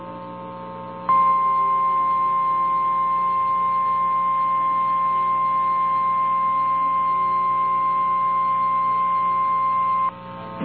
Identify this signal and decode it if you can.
Emergency Alert System (SAME) tones (the 1050 Hz attention tone). It is heard from 1.1 to 10.1 s.
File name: eas / KNCYKBIE / KNCYKBIE